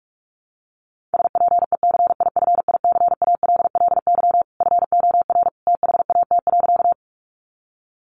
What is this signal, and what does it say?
Morse code, heard as a keyed tone, about 44 wpm, 700 Hz. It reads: SPECIFICALLY FOR THAT.
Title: Sentences from Top 1100 Words 45wpm